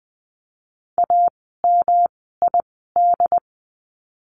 Morse code, heard as a keyed tone, about 20 wpm, spaced as for 14 wpm, 700 Hz. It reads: AMID
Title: 108 Lesson 18f - Letter m - Words 20wpm